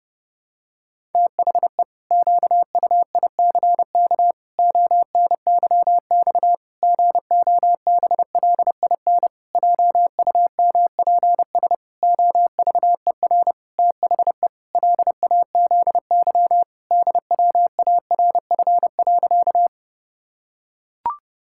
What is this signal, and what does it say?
Morse code, heard as a keyed tone, about 30 wpm, 700 Hz. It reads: THE QUICK ONYX GOBLIN JUMPS OVER THE LAZY DWARF.
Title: Pangram Challenge v2 - 30wpm